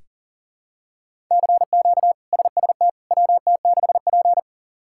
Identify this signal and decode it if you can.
Morse code, heard as a keyed tone, about 40 wpm, 700 Hz. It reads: CQ SST WT6P
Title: K1USN SST QSO Practice - 40wpm